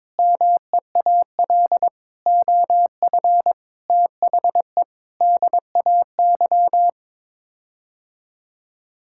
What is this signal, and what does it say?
Morse code, heard as a keyed tone, about 22 wpm, 700 Hz. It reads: MEAL OF THE DAY
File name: Sets of 9 Words - 22wpm